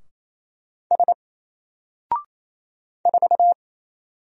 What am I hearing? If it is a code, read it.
Morse code, heard as a keyed tone, about 28 wpm, 700 Hz. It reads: S 4
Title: Single Letter-Number Alt 28wpm